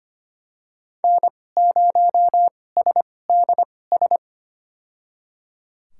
Morse code, heard as a keyed tone, about 25 wpm, spaced as for 18 wpm, 700 Hz. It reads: N0SDS
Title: USA Call Signs - 5 Characters - 2x Character Spacing - 25wpm